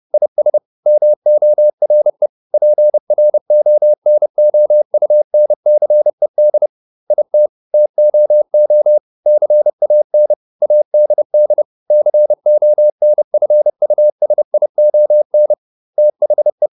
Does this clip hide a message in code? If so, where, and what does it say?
Yes: Morse code from 0.1 to 16.7 s, heard as a keyed tone, about 30 wpm, 600 Hz. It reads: IS MORE PRONOUNCED IT TOO CAN ADD CONFUSION THE